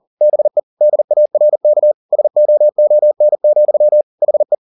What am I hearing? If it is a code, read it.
Morse code, heard as a keyed tone, about 40 wpm, 600 Hz. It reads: BE DARK SOON, HE
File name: the old man and the sea 2 40wpm